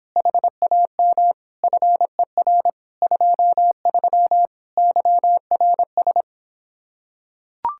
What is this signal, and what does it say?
Morse code, heard as a keyed tone, about 26 wpm, 700 Hz. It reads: HAM FER 23 YRS